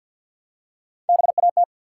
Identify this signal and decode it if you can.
Morse code, heard as a keyed tone, about 47 wpm, 700 Hz. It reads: BAT
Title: Three Letter Words - 50wpm